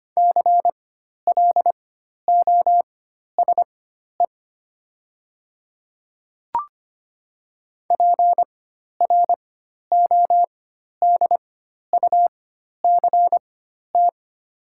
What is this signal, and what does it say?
Morse code, heard as a keyed tone, about 25 wpm, spaced as for 12 wpm, 690 Hz. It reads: CLOSE PRODUCT
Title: 4x Character Spacing - Top 500 Words - 25wpm